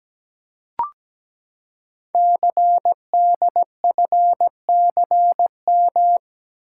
Morse code, heard as a keyed tone, about 17 wpm, 700 Hz. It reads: CDFCM
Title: Code Groups - 5 Letters - 17wpm